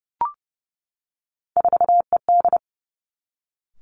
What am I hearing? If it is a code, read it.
Morse code, heard as a keyed tone, about 30 wpm, 700 Hz. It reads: 4ED
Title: Three Challenging Letter-Number Combinations 30wpm